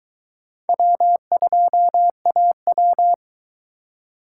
Morse code, heard as a keyed tone, about 23 wpm, 700 Hz. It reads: W2AW